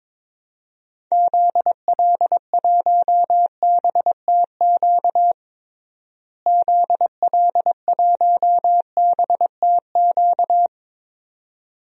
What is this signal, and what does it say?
Morse code, heard as a keyed tone, about 22 wpm, 700 Hz. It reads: ZL1BTQ ZL1BTQ
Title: DXCC-100 Call Signs - 6 Characters - Repeated 2x - 22wpm